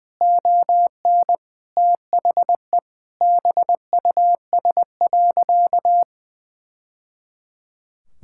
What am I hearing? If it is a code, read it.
Morse code, heard as a keyed tone, about 19 wpm, 700 Hz. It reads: ON THE BUS.
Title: Sentences from Top 1000 Words - Review Part 6 - 20wpm